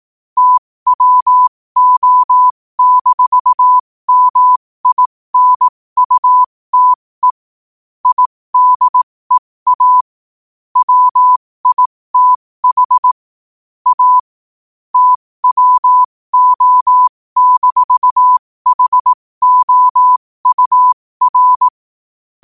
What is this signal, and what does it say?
Morse code, heard as a keyed tone, about 18 wpm, spaced as for 15 wpm, 1000 Hz. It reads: TWO-MINUTE IDEA WITH A TWO-HOUR
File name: qotdfiles / 15wpm